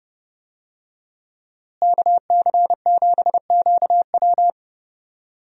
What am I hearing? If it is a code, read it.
Morse code, heard as a keyed tone, about 30 wpm, 700 Hz. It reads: KC7QW
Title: USA Call Signs - 5 Characters - Repeated 3x - 30wpm